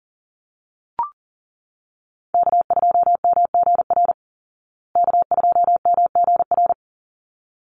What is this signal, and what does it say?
Morse code, heard as a keyed tone, about 40 wpm, 700 Hz. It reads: K2MGR K2MGR
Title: Call Signs Repeated 3x - 40wpm